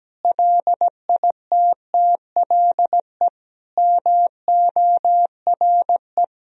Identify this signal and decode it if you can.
Morse code, heard as a keyed tone, about 17 wpm, 700 Hz. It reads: LITTLE MORE
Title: Sentences from Top 1000 Words - Review Part 2 - 17wpm